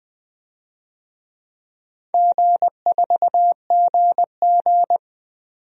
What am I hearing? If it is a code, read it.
Morse code, heard as a keyed tone, about 20 wpm, 700 Hz. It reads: G4GG
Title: DXCC-100 Call Signs - 4 Characters - 20wpm